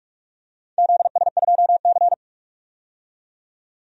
Morse code, heard as a keyed tone, about 42 wpm, 700 Hz. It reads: ZS2C
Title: DXCC-100 Call Signs - 4 Characters - 45wpm